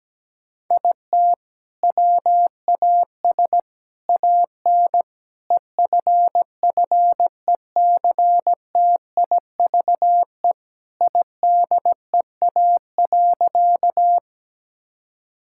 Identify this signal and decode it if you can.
Morse code, heard as a keyed tone, about 17 wpm, 700 Hz. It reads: IT WAS AN EFFECTIVE IDEA.
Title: Sentences from Top 1000 Words - Review Part 6 - 17wpm